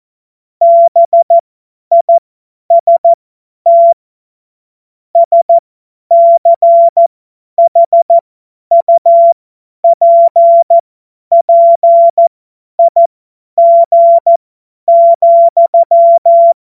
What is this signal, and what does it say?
Morse code, heard as a keyed tone, about 14 wpm, spaced as for 10 wpm, 680 Hz. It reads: BIST SCHUPPIG,
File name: Morsekurs / Witze / Witz 25. Oktober 2022 50 F70